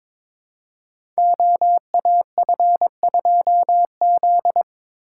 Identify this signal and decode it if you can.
Morse code, heard as a keyed tone, about 22 wpm, 700 Hz. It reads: OAF2Z
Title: Code Groups - 5 Letter:Numbers - 22wpm